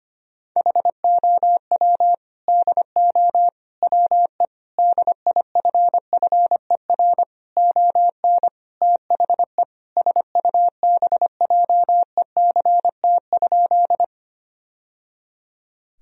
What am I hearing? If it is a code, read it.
Morse code, heard as a keyed tone, about 25 wpm, 700 Hz. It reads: HOW DO WE DIFFER ON THE SUBJECT?
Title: Sentences from Top 1000 Words - Review Part 1 - 25wpm